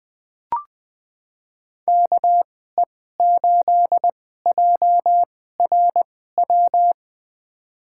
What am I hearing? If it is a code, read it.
Morse code, heard as a keyed tone, about 20 wpm, spaced as for 14 wpm, 700 Hz. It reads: KE8JRW
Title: USA Call Signs - 2x Character Spacing - 20wpm